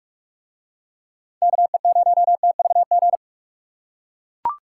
Morse code, heard as a keyed tone, about 43 wpm, 700 Hz. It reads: KE0TVG